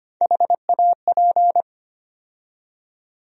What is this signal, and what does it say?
Morse code, heard as a keyed tone, about 25 wpm, 700 Hz. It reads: HAP
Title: USA Call Sign Postfixes - 3 Letters 25wpm